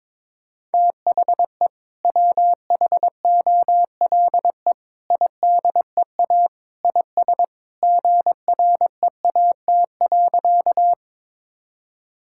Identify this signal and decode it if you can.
Morse code, heard as a keyed tone, about 22 wpm, 700 Hz. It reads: THE WHOLE IDEA IS GREAT.
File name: practice / 22wpm / Sentences with 2-Word Phrases - 22wpm